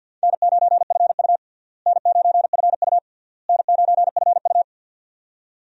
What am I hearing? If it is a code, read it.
Morse code, heard as a keyed tone, about 53 wpm, 700 Hz. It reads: N9FU N9FU N9FU
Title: USA Call Signs - 4 Characters - Repeated 3x - 50wpm